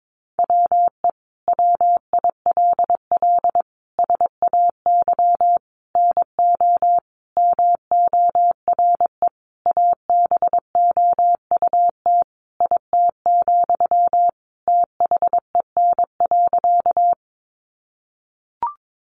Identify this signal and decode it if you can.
Morse code, heard as a keyed tone, about 22 wpm, 700 Hz. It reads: WE WILL SAY NO MORE ABOUT IT, THEN.
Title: Sentences from Top 200 Words 22wpm